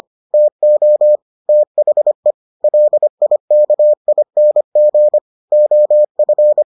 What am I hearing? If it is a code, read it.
Morse code, heard as a keyed tone, about 25 wpm, 600 Hz. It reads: TO THE LIKING OF